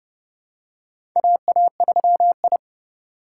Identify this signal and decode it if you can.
Morse code, heard as a keyed tone, about 30 wpm, 700 Hz. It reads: AA3I
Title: USA Call Signs - 3x Character Spacing - 30wpm